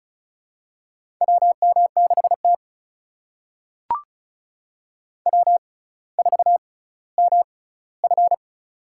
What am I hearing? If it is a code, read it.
Morse code, heard as a keyed tone, about 36 wpm, spaced as for 18 wpm, 700 Hz. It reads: WM6T W4MF
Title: USA Call Signs - 6x Character Spacing - 35wpm